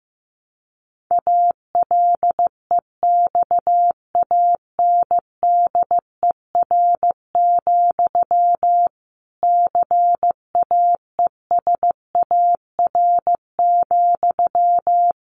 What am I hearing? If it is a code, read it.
Morse code, heard as a keyed tone, about 15 wpm, 700 Hz. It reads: ALEXANDER, CAESAR,
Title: A Sound Of Thunder 15wpm